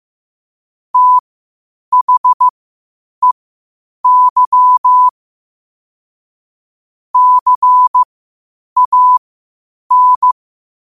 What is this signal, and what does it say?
Morse code, heard as a keyed tone, about 15 wpm, spaced as for 8 wpm, 1000 Hz. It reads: THEY CAN
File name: qotdfiles / 8wpm